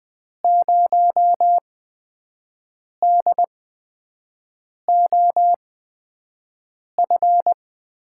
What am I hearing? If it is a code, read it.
Morse code, heard as a keyed tone, about 20 wpm, spaced as for 5 wpm, 700 Hz. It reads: 0DOF
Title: USA Call Signs - 6 Characters - 8x Character Spacing - 20wpm